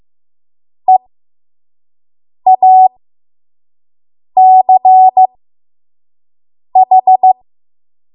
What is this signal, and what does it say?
Morse code, heard as a keyed tone, about 15 wpm, spaced as for 5 wpm, 750 Hz. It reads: EACH
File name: Morse / 05WPM